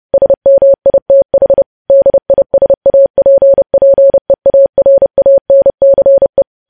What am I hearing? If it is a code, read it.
Morse code, heard as a keyed tone, about 30 wpm, 550 Hz. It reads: SMITH DISAPPEARANCE
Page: news headlines in morse code at 30 wpm